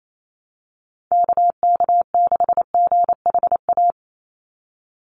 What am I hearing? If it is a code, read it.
Morse code, heard as a keyed tone, about 28 wpm, 700 Hz. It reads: KK6GHA
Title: Call Signs - 28wpm